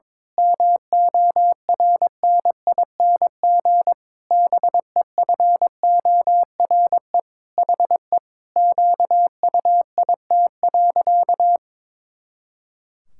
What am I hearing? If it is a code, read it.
Morse code, heard as a keyed tone, about 22 wpm, 700 Hz. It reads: MORNING BEFORE HE QUIT.
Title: Pangram Challenge v2 - 22wpm